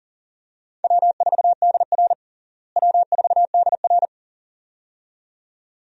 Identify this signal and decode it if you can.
Morse code, heard as a keyed tone, about 40 wpm, 700 Hz. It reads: W4DR W4DR
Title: USA Call Signs - 4 Characters - Repeated 2x - 40wpm